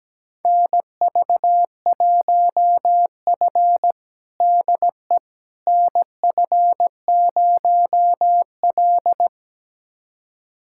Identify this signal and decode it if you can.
Morse code, heard as a keyed tone, about 17 wpm, 700 Hz. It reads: NV1F DE NF0L
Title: USA Callsign DE Callsign - 4 Characters - 17wpm